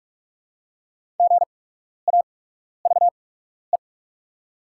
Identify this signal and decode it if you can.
Morse code, heard as a keyed tone, about 44 wpm, spaced as for 12 wpm, 700 Hz. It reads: GAVE
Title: 8x Character Spacing - Top 500 Words - 45wpm